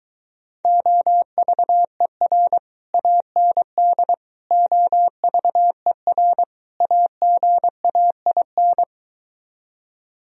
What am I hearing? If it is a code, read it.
Morse code, heard as a keyed tone, about 23 wpm, 700 Hz. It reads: OVER AND OVER AGAIN